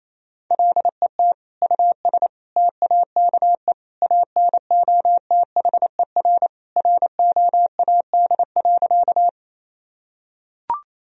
Morse code, heard as a keyed tone, about 28 wpm, 700 Hz. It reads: LET US TAKE ANOTHER ROAD.
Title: Sentences from Top 1000 Words - Review Part 2 - 28wpm